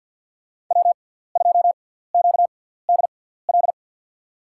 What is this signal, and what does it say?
Morse code, heard as a keyed tone, about 51 wpm, spaced as for 17 wpm, 700 Hz. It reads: W2QDL
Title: USA Call Signs - 5 Characters - 6x Character Spacing - 50wpm